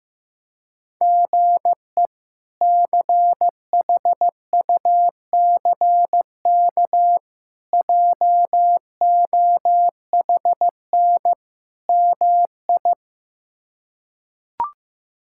Morse code, heard as a keyed tone, about 15 wpm, 700 Hz. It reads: GE CHUCK JOHN MI
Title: K1USN SST QSO Practice - 15wpm